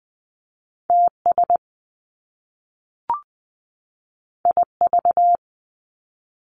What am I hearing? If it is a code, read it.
Morse code, heard as a keyed tone, about 20 wpm, 700 Hz. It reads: T S I V
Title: Top Two Letter Combinations in Words 20wpm